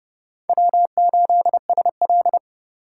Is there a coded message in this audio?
Yes: Morse code from 0.5 to 2.4 s, heard as a keyed tone, about 39 wpm, 700 Hz. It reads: W8SL